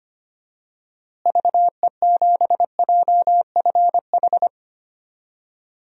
Morse code, heard as a keyed tone, about 25 wpm, 700 Hz. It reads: VE7JFH
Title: DXCC-100 Call Signs - 6 Characters - Repeated 2x - 25wpm